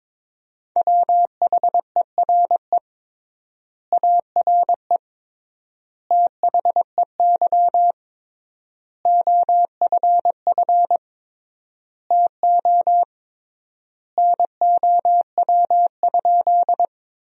Morse code, heard as a keyed tone, about 22 wpm, 700 Hz. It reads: WHERE ARE THEY OFF TO NOW?
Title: 3x Word-Spacing - Sentences from Top 200 Words - 22wpm